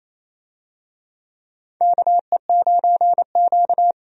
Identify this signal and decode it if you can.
Morse code, heard as a keyed tone, about 28 wpm, 700 Hz. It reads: KE9Q
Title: USA Call Signs - 4 Characters - 28wpm